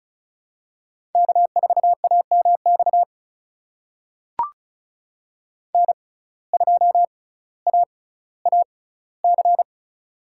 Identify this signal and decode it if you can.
Morse code, heard as a keyed tone, about 36 wpm, spaced as for 18 wpm, 700 Hz. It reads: K4AMX N2AAC
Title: USA Call Signs - 5 Characters - 6x Character Spacing - 35wpm